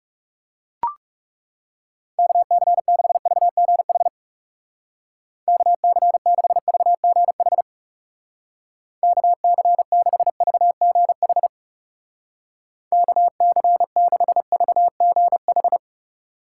Morse code, heard as a keyed tone, about 40 wpm, 700 Hz. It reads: KC6VGH KC6VGH KC6VGH KC6VGH